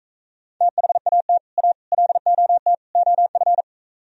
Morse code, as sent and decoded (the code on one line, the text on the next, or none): - .... .- - / .- / .-.. --- - / --- ..-.
THAT A LOT OF